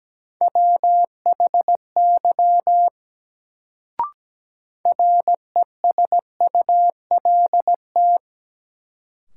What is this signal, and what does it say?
Morse code, heard as a keyed tone, about 17 wpm, 700 Hz. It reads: WHY RESULT
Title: Sets of 3 Words - New Word Drill - 17wpm